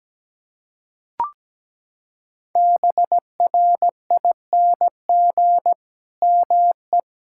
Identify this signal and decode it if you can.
Morse code, heard as a keyed tone, about 17 wpm, 700 Hz. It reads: BRING ME
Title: Sentences from Top 1600 Words 17wpm